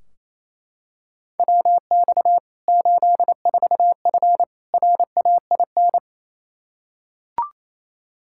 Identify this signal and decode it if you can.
Morse code, heard as a keyed tone, about 28 wpm, 700 Hz. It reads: WX 84F RAIN
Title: K1USN SST QSO Practice 28wpm